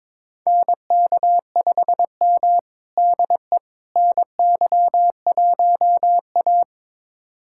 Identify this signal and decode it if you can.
Morse code, heard as a keyed tone, about 22 wpm, 700 Hz. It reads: NK5M DE NY1A
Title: USA Callsign DE Callsign - 4 Characters - 22wpm